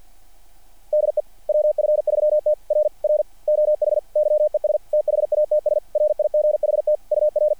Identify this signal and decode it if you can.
Morse code, heard as a keyed tone, about 50 wpm, 600 Hz. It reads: DE WF3T R R OH YES THATS RIGHT UR
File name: archived / CRSnet